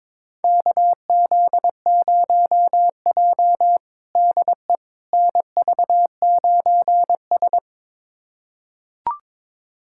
Morse code, heard as a keyed tone, about 22 wpm, 700 Hz. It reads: KZ0J DE NV9S
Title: USA Callsign DE Callsign - 4 Characters - 22wpm